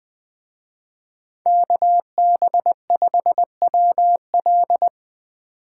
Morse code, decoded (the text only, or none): KB5WL